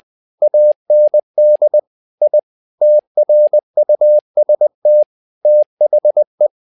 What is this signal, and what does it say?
Morse code, heard as a keyed tone, about 20 wpm, 600 Hz. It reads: AND I TRUST THE
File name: tower trouble 20wpm